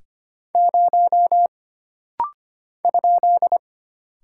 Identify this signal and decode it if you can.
Morse code, heard as a keyed tone, about 25 wpm, 700 Hz. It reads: 0 ?
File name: Single Character (Letter:Number:Punctuation:Prosign) - Rapid-Fire - 25wpm